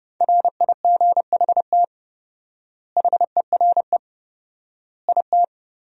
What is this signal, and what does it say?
Morse code, heard as a keyed tone, about 30 wpm, 700 Hz. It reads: RIGHT HERE IT
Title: 4x Word-Spacing - Sentences from Top 300 Words - 30wpm